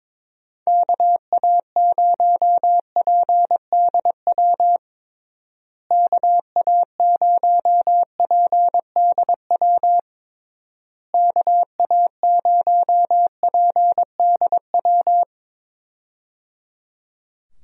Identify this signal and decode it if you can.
Morse code, heard as a keyed tone, about 22 wpm, 700 Hz. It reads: KA0PDW KA0PDW KA0PDW
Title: USA Call Signs - 6 Characters - Repeated 3x - 22wpm